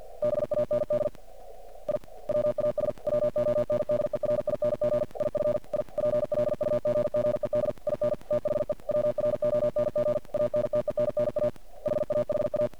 Morse code, heard as a keyed tone, about 50 wpm, 600 Hz, with noise. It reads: BAND I WAS WONDERING IF I PLUGGED IN THE WRONG ANTENNA HAHA
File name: archived / CRSnet